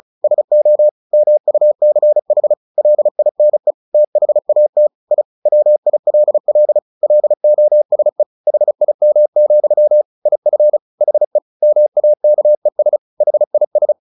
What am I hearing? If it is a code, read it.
Morse code, heard as a keyed tone, about 34 wpm, 600 Hz. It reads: SO MUCH LINE THAT I WILL LOSE HIM, IF HE MAKES HIS
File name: the old man and the sea 2 35wpm